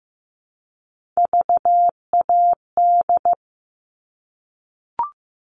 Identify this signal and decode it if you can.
Morse code, heard as a keyed tone, about 15 wpm, 700 Hz. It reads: VAD
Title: All Cut Numbers - 3-Digit 15wpm